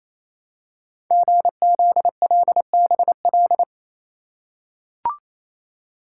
Morse code, heard as a keyed tone, about 28 wpm, 700 Hz. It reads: GZLBL
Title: Code Groups - 5 Letters - 28wpm